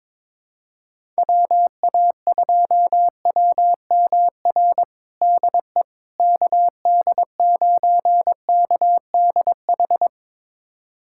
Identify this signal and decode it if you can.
Morse code, heard as a keyed tone, about 21 wpm, 700 Hz. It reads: WA2WMR DE KD9KDH
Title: USA Callsign DE Callsign - 6 Characters - 22wpm